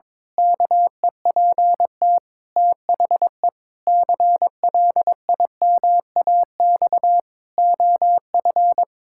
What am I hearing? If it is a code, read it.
Morse code, heard as a keyed tone, about 22 wpm, 700 Hz. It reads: KEPT THE CLIMAX OF
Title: Pangram Challenge v2 - 22wpm